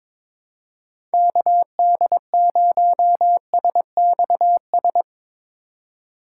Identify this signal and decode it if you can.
Morse code, heard as a keyed tone, about 22 wpm, 700 Hz. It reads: KD0SXS
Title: USA Call Signs - 6 Characters - 3x Character Spacing - 22wpm